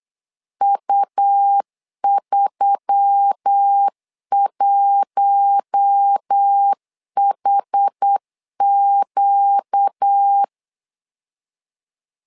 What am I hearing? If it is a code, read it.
Morse code, heard as a keyed tone, about 8 wpm, 800 Hz. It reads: U31HQ
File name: LEZIONI / 4 Q1S3 8wpm 3037